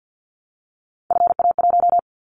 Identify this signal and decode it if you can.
Morse code, heard as a keyed tone, about 50 wpm, 700 Hz. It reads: FA1
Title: Callsign Prefixes for Top 50 DXCC - 50wpm